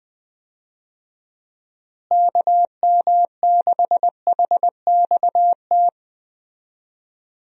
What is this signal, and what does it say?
Morse code, heard as a keyed tone, about 21 wpm, 700 Hz. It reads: KM6HXT
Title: USA Call Signs - 6 Characters - Repeated 2x - 20wpm